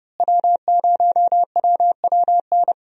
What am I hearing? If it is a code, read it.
Morse code, heard as a keyed tone, about 30 wpm, 700 Hz. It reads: W0WWN
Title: USA Call Signs - 5 Characters - Repeated 3x - 30wpm